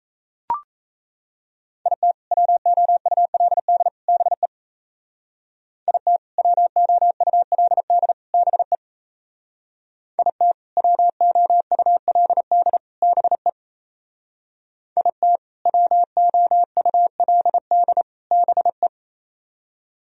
Morse code, heard as a keyed tone, about 34 wpm, 700 Hz. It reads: IT WOULD BE IT WOULD BE IT WOULD BE IT WOULD BE
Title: Speed-Racing - Sets of 3 Words 28wpm